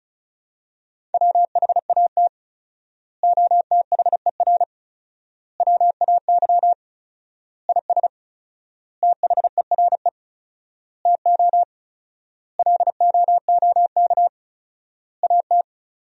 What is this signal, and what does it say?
Morse code, heard as a keyed tone, about 36 wpm, 700 Hz. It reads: WHAT OTHER WAY IS THERE TO LOOK AT
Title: 4x Word-Spacing - Sentences from Top 100 Words - 35wpm